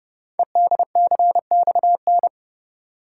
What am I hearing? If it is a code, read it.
Morse code, heard as a keyed tone, about 28 wpm, 700 Hz. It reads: EDCXN
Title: Code Groups - 5 Letters - 30wpm